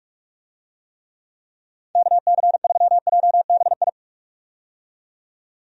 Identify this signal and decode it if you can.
Morse code, heard as a keyed tone, about 46 wpm, 700 Hz. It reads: KC3JBI
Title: USA Call Signs - 6 Characters - 45wpm